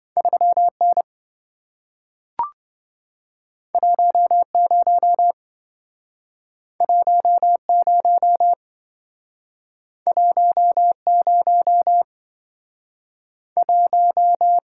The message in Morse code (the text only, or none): 3N 10 10 10 1